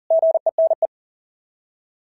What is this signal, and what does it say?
Morse code, heard as a keyed tone, about 40 wpm, 660 Hz. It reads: GENE
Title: SOTA QSO Practice 40wpm